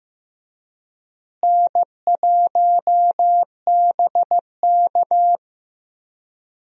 Morse code, heard as a keyed tone, about 15 wpm, 700 Hz. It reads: N1BK
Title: USA Call Signs - 4 Characters - Repeated 2x - 15wpm